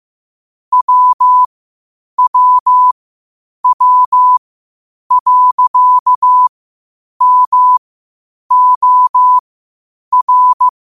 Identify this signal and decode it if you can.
Morse code, heard as a keyed tone, about 15 wpm, spaced as for 8 wpm, 1000 Hz. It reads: WWW.MOR
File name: qotdfiles / 8wpm